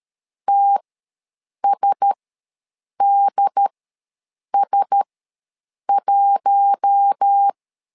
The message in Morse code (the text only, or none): TSDS1